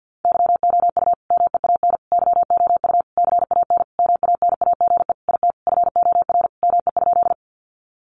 Morse code, heard as a keyed tone, about 50 wpm, 700 Hz. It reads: YOU MEAN YOU CAN MANAGE IT FOR ME?
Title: Sentences from Top 1400 Words 50wpm